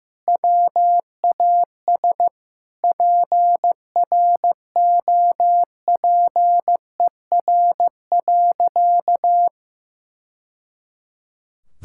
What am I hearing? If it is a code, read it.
Morse code, heard as a keyed tone, about 16 wpm, 700 Hz. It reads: WAS PROPER.
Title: Sentences from Top 1000 Words - Review Part 5 - 15wpm